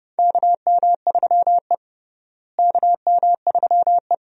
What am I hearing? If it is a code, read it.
Morse code, heard as a keyed tone, about 30 wpm, 700 Hz. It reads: KM3E KM3E
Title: USA Call Signs - 4 Characters - Repeated 3x - 30wpm